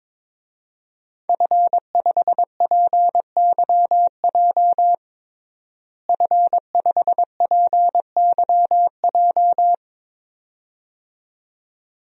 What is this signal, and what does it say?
Morse code, heard as a keyed tone, about 21 wpm, 700 Hz. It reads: F5PYJ F5PYJ
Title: DXCC-100 Call Signs - 5 Characters - Repeated 2x - 22wpm